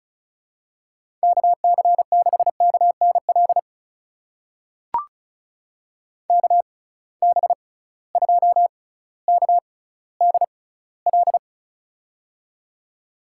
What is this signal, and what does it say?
Morse code, heard as a keyed tone, about 36 wpm, spaced as for 18 wpm, 700 Hz. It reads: KC6KNL KB2KDL
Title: USA Call Signs - 6 Characters - 6x Character Spacing - 35wpm